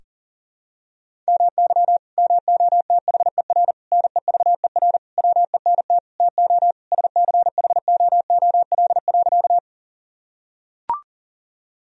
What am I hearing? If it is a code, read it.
Morse code, heard as a keyed tone, about 40 wpm, 700 Hz. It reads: MY MOTHER NEVER WENT TO SCHOOL.
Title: Sentences from Top 1000 Words - Review Part 1 - 40wpm